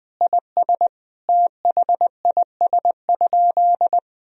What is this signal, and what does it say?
Morse code, heard as a keyed tone, about 21 wpm, 700 Hz. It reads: IS THIS?
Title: Sentences from Top 1000 Words - Review Part 2 - 20wpm